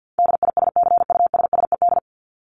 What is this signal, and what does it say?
Morse code, heard as a keyed tone, about 50 wpm, 700 Hz. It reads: DISCUSSED
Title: ED Suffix - Top 100 Words with Base Words 50wpm